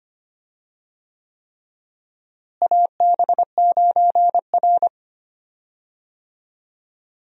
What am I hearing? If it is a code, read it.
Morse code, heard as a keyed tone, about 25 wpm, 700 Hz. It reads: AB9R
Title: USA Call Signs - 4 Characters - Repeated 3x - 25wpm